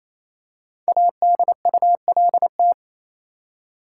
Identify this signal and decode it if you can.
Morse code, heard as a keyed tone, about 28 wpm, 700 Hz. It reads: ADULT
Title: Five Letter Words - 28wpm